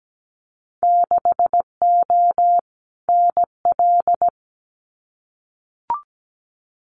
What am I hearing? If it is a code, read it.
Morse code, heard as a keyed tone, about 17 wpm, 700 Hz. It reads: 6O NL
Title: Winter Field Day Exchanges Encore 17wpm